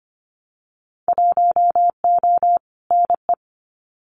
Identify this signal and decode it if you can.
Morse code, heard as a keyed tone, about 25 wpm, 700 Hz. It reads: 1O NE
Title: Winter Field Day Exchanges 25wpm